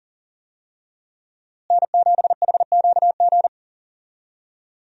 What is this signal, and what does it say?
Morse code, heard as a keyed tone, about 40 wpm, 700 Hz. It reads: N7HQG
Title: USA Call Signs - 5 Characters - Repeated 3x - 40wpm